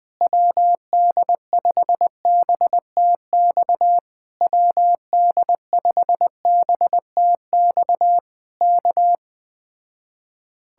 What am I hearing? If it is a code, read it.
Morse code, heard as a keyed tone, about 20 wpm, 700 Hz. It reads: WD5BTX WD5BTX K
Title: Speed-Racing Common QSO Phrases Encore - 17wpm